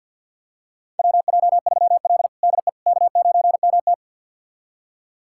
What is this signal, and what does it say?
Morse code, heard as a keyed tone, about 49 wpm, 700 Hz. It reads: WJ3L DE K9MT
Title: USA Callsign DE Callsign - 4 Characters - 50wpm